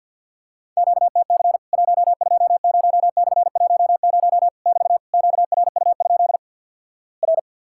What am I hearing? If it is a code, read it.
Morse code, heard as a keyed tone, about 50 wpm, 700 Hz. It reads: QTC 120/10 <BT> QRU? R
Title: Worked All Europe DX Contest - QTC Exchange with Repeats - 50wpm